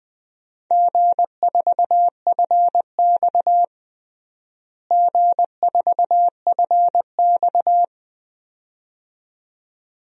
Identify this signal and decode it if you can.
Morse code, heard as a keyed tone, about 21 wpm, 700 Hz. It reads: G4FX G4FX
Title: DXCC-100 Call Signs - 4 Characters - Repeated 2x - 20wpm